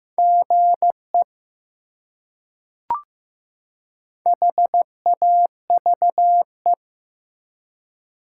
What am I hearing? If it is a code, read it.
Morse code, heard as a keyed tone, about 15 wpm, 700 Hz. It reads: GE HAVE
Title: Top 180 QSO Elements 15wpm